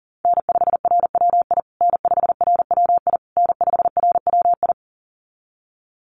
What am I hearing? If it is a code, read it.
Morse code, heard as a keyed tone, about 40 wpm, 700 Hz. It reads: N5RWI N5RWI N5RWI
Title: USA Call Signs Repeated 3x 40wpm